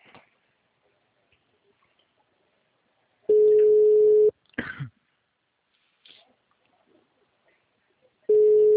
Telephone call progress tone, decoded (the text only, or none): ringback tone